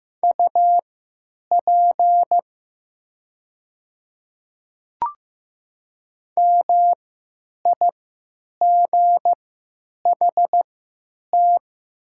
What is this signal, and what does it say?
Morse code, heard as a keyed tone, about 15 wpm, spaced as for 9 wpm, 700 Hz. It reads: UP MIGHT
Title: 3x Character Spacing - Top 300 Words - 15wpm